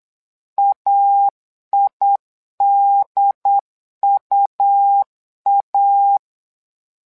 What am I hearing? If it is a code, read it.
Morse code, heard as a keyed tone, about 8 wpm, 800 Hz. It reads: AIDUA
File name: LEZIONI / 2 UHDN / 2 UHD 8wpm 2258